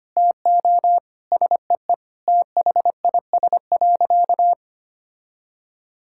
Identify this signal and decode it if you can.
Morse code, heard as a keyed tone, about 25 wpm, 700 Hz. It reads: TO SEE THIS.
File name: practice / 25wpm / Sentences with 2-Word Phrases - 25wpm